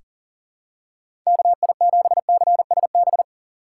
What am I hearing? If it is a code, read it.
Morse code, heard as a keyed tone, about 37 wpm, 700 Hz. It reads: KI7CSB
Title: USA Call Signs - 3x Character Spacing - 40wpm